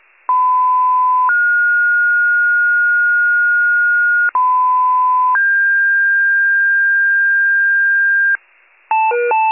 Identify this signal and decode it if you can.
Two-tone page, heard as a one-second tone, then a three-second tone, about 1010 then 1430 Hz, about 1010 then 1600 Hz, with noise. It is heard from 0.3 to 8.4 s.
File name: wcalert